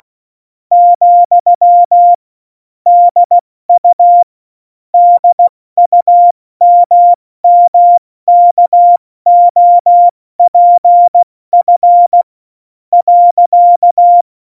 Morse code, heard as a keyed tone, about 16 wpm, 700 Hz. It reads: , DU DUMMKOPF .
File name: Morsekurs / Witze / Witz 11. September 2018, 70 F80